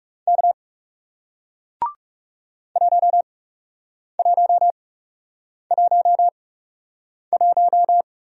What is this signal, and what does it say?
Morse code, heard as a keyed tone, about 41 wpm, spaced as for 8 wpm, 700 Hz. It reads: K 1111